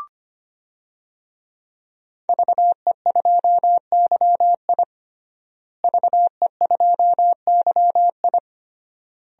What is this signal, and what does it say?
Morse code, heard as a keyed tone, about 25 wpm, 700 Hz. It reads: VE2YI VE2YI
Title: DXCC-100 Call Signs - 5 Characters - Repeated 3x - 25wpm